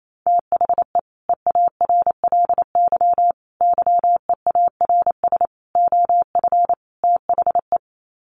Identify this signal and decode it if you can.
Morse code, heard as a keyed tone, about 28 wpm, 700 Hz. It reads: THE EARLY YEARS OF THE
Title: Sets of 5 Words Encore 28wpm